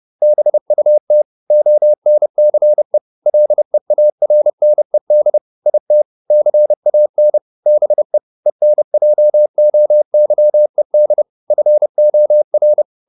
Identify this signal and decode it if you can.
Morse code, heard as a keyed tone, about 30 wpm, 600 Hz. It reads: BUT ONCE LEARNED IT CAN BE ENJOYED FOR